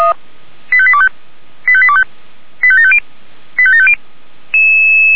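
Five-tone selective call (ZVEI), heard as quick runs of five tones, tones of about 70 ms. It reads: 85825 85825 85750 85750
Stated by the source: ZVEI five-tone call received by SDR.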